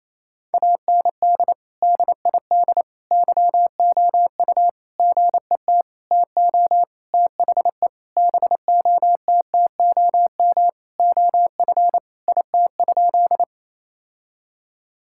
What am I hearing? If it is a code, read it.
Morse code, heard as a keyed tone, about 27 wpm, 700 Hz. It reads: AND DID YOU GET TO THE BOTTOM OF IT?
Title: Sentences from Top 1000 Words - Review Part 6 - 28wpm